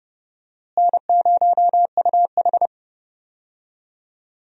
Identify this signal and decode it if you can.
Morse code, heard as a keyed tone, about 30 wpm, 700 Hz. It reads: N0UH
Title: USA Call Signs - 4 Characters - Repeated 2x - 30wpm